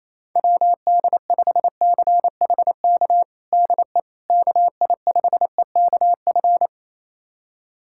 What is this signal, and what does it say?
Morse code, heard as a keyed tone, about 28 wpm, 700 Hz. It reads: WD5CHK DE KI5EKF
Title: USA Callsign DE Callsign 28wpm